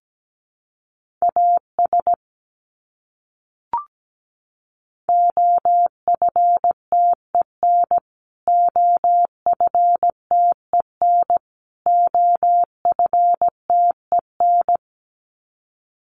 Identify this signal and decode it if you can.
Morse code, heard as a keyed tone, about 17 wpm, 700 Hz. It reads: AS OFTEN OFTEN OFTEN
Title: Top 500 Words Repeated 17wpm